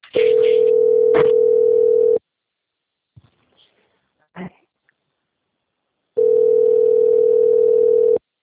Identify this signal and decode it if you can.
Ringback tone (North America), 440 + 480 Hz.